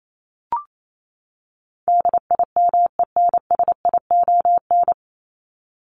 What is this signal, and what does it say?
Morse code, heard as a keyed tone, about 28 wpm, 700 Hz. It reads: DIMENSION
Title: Top 1501-1750 Words 28wpm